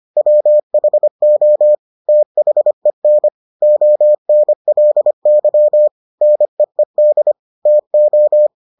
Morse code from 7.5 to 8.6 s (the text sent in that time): TO